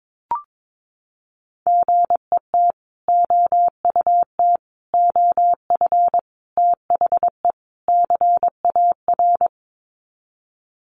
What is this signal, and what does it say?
Morse code, heard as a keyed tone, about 22 wpm, 700 Hz. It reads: GET OUT OF THE CAR
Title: Sets of 5 Words Encore2 22wpm